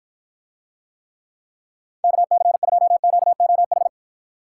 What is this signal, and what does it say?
Morse code, heard as a keyed tone, about 53 wpm, 700 Hz. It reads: KC2QOH